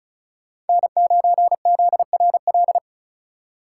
Morse code, heard as a keyed tone, about 34 wpm, 700 Hz. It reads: N9ZRL
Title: USA Call Signs - 3x Character Spacing - 35wpm